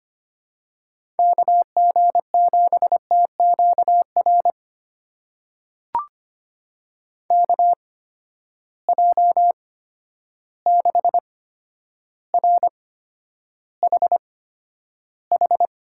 Morse code, heard as a keyed tone, about 25 wpm, spaced as for 11 wpm, 700 Hz. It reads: KG7TQR KJ6RHH
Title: USA Call Signs - 6 Characters - 8x Character Spacing - 25wpm